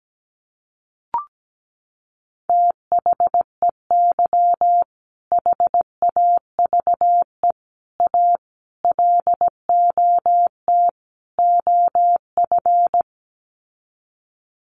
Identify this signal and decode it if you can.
Morse code, heard as a keyed tone, about 17 wpm, 700 Hz. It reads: THEY HAVE A LOT OF
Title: Sets of 5 Words Encore2 17wpm